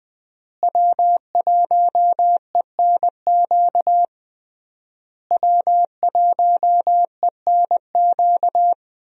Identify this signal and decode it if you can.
Morse code, heard as a keyed tone, about 20 wpm, 700 Hz. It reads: W1ENQ W1ENQ
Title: USA Call Signs - 5 Characters - Repeated 3x - 20wpm